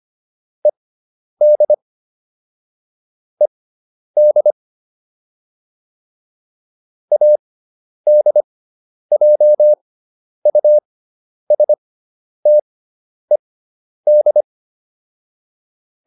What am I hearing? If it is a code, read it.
Morse code, heard as a keyed tone, about 25 wpm, spaced as for 10 wpm, 600 Hz. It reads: ED ED ADJUSTED